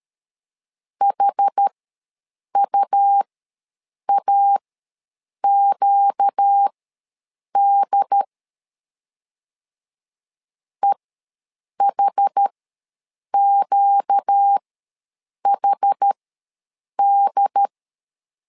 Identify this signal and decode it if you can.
Morse code, heard as a keyed tone, about 13 wpm, spaced as for 7 wpm, 800 Hz. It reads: HUAQD EHQHD
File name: LEZIONI / 4 Q ws 12wpm 1912